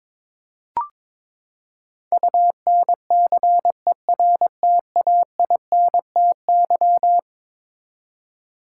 Morse code, heard as a keyed tone, about 22 wpm, 700 Hz. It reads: UNCERTAINTY
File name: practice / 22wpm / Eleven Letter Words - 22wpm